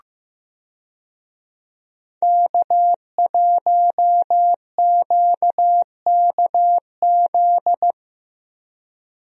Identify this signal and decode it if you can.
Morse code, heard as a keyed tone, about 15 wpm, 700 Hz. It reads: K1QKZ
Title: USA Call Signs - 5 Characters - Repeated 2x - 15wpm